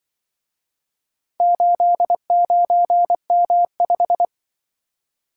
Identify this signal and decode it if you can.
Morse code, heard as a keyed tone, about 25 wpm, 700 Hz. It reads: 89M5
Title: Speed-Racing - Four Letter-Number Combinations - 20wpm